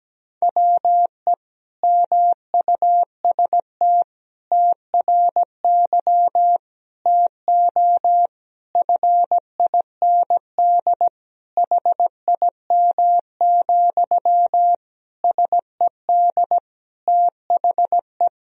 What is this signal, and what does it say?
Morse code, heard as a keyed tone, about 17 wpm, 700 Hz. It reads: WE MUST TRY TO FIND HIM, SED THE
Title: Sentences from Top 400 Words and Abbreviations 17wpm